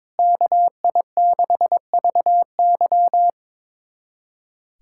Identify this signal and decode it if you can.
Morse code, heard as a keyed tone, about 22 wpm, 700 Hz. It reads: KI6VY
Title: USA Callsign DE Callsign 22wpm